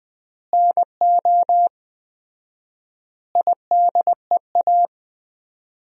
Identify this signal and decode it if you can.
Morse code, heard as a keyed tone, about 20 wpm, 700 Hz. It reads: NO IDEA
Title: 4x Word-Spacing - Sentences from Top 300 Words - 20wpm